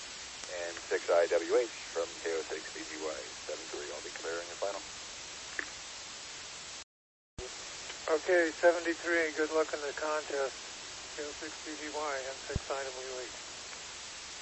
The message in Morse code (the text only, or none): none